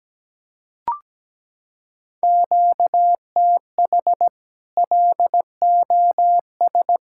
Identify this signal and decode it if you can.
Morse code, heard as a keyed tone, about 17 wpm, 700 Hz. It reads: QTH LOS
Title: Practice QSO - Exchange 1 - Beginning - 17wpm